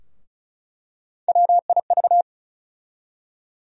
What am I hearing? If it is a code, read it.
Morse code, heard as a keyed tone, about 36 wpm, 700 Hz. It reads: WIV
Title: USA Call Sign Postfixes - 3 Letters 35wpm